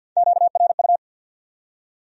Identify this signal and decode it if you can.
Morse code, heard as a keyed tone, about 64 wpm, 700 Hz. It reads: QRU